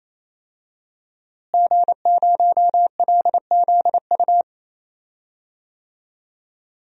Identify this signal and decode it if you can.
Morse code, heard as a keyed tone, about 28 wpm, 700 Hz. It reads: G0LZU
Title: DXCC-100 Call Signs - 5 Characters - Repeated 3x - 28wpm